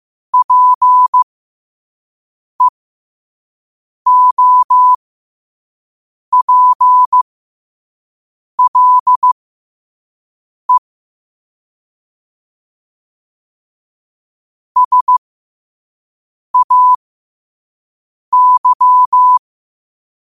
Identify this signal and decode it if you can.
Morse code, heard as a keyed tone, about 15 wpm, spaced as for 5 wpm, 1000 Hz. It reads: PEOPLE SAY
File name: qotdfiles / 5wpm